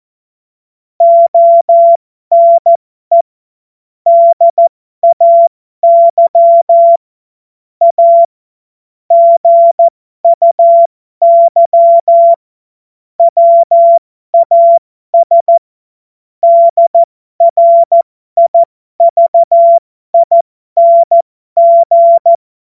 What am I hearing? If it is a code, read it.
Morse code, heard as a keyed tone, about 14 wpm, spaced as for 12 wpm, 680 Hz. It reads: ONE DAY A GUY WAS DRIVING
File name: Morsekurs / Witze